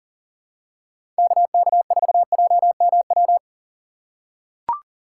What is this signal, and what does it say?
Morse code, heard as a keyed tone, about 40 wpm, 700 Hz. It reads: KK4JMW
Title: USA Call Signs - 6 Characters - 6x Character Spacing - 40wpm